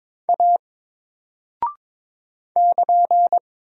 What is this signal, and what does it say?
Morse code, heard as a keyed tone, about 22 wpm, 700 Hz. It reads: A <KN>
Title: Single Character (Letter:Number:Punctuation:Prosign) 22wpm